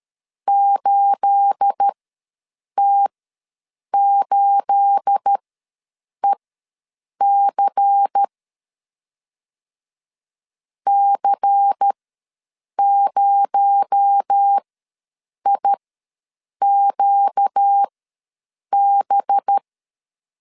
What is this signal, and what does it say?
Morse code, heard as a keyed tone, about 13 wpm, spaced as for 7 wpm, 800 Hz. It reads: 8T8EC C0IQB
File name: LEZIONI / 9 28 ws 12wpm 3354